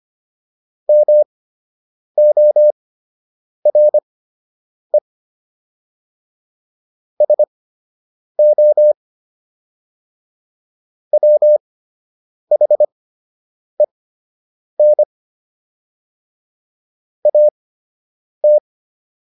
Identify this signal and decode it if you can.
Morse code, heard as a keyed tone, about 25 wpm, spaced as for 8 wpm, 600 Hz. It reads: MORE SO WHEN AT